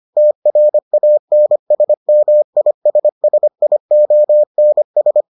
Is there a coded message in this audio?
Yes: Morse code from 0.2 to 5.2 s, heard as a keyed tone, about 25 wpm, 600 Hz. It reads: TRANSMISSIONS